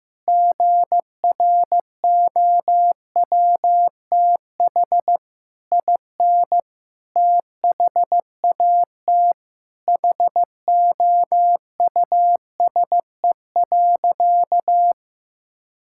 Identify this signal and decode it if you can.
Morse code, heard as a keyed tone, about 15 wpm, 700 Hz. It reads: GROWTH IN THAT HOUSE.
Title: Sentences from Top 1000 Words - Review Part 5 - 15wpm